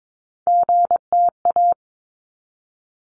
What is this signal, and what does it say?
Morse code, heard as a keyed tone, about 22 wpm, 700 Hz. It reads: GTA
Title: ARRL Sections 22wpm